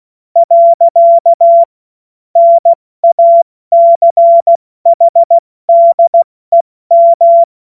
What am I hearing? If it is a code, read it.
Morse code, heard as a keyed tone, about 16 wpm, 680 Hz. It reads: . NACHDEM
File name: Morsekurs / Witze / Witz 23. Dezember 2025 70 F80